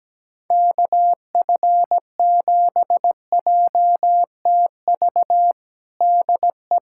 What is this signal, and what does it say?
Morse code, heard as a keyed tone, about 17 wpm, 700 Hz. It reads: KF7JTV DE
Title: USA Callsign DE Callsign 17wpm